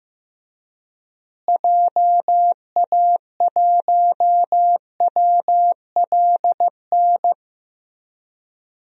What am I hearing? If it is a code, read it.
Morse code, heard as a keyed tone, about 15 wpm, 700 Hz. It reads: JA1WLN
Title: DXCC-100 Call Signs - 6 Characters - Repeated 2x - 15wpm